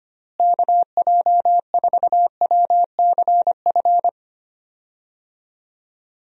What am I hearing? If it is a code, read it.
Morse code, heard as a keyed tone, about 25 wpm, 700 Hz. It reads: KJ4WCF
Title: USA Call Signs - 6 Characters - Repeated 3x - 25wpm